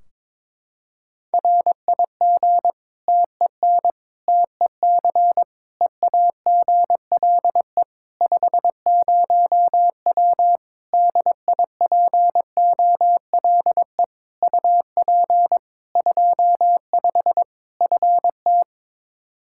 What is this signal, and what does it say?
Morse code, heard as a keyed tone, about 22 wpm, 700 Hz. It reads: RIG TEN TEC EAGLE 50W DIPOLE UP 25 FT
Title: Practice QSO - Exchange 2 - Info - 22wpm